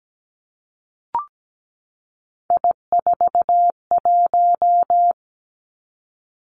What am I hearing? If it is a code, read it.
Morse code, heard as a keyed tone, about 17 wpm, 700 Hz. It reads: I41
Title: Three Challenging Letter-Number Combinations 17wpm